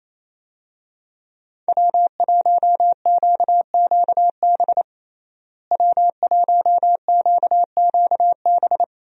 Morse code, heard as a keyed tone, about 28 wpm, 700 Hz. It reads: W1QQB W1QQB
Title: USA Call Signs - 5 Characters - Repeated 2x - 28wpm